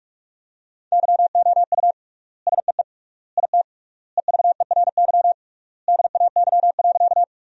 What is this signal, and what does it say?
Morse code, heard as a keyed tone, about 43 wpm, 700 Hz. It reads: YOU SEE IT EVERY DAY.
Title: 3x Word-Spacing - Sentences from Top 200 Words - 45wpm